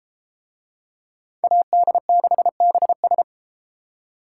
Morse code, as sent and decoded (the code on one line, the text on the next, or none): .- -.. -.... -... ...
AD6BS